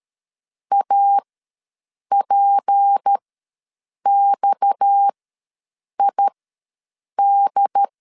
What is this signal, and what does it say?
Morse code, heard as a keyed tone, about 13 wpm, spaced as for 7 wpm, 800 Hz. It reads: APXID
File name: LEZIONI / 9 2 ws 12wpm 3406